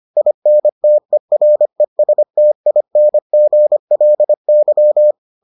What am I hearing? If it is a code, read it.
Morse code, heard as a keyed tone, about 25 wpm, 600 Hz. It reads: INTERESTINGLY